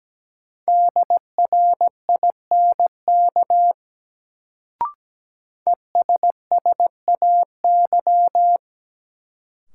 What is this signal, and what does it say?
Morse code, heard as a keyed tone, about 17 wpm, 700 Hz. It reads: DRINK ESSAY
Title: Sentences from Top 1500 Words - New Word Drill - 17wpm